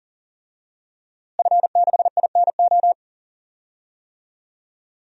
Morse code, heard as a keyed tone, about 40 wpm, 700 Hz. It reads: F6INO
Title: DXCC-100 Call Signs - 5 Characters - Repeated 2x - 40wpm